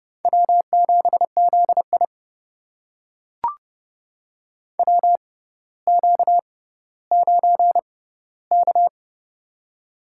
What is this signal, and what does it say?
Morse code, heard as a keyed tone, about 30 wpm, spaced as for 15 wpm, 690 Hz. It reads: W7ZI WQ9K
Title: USA Call Signs - 4 Characters - 6x Character Spacing - 30wpm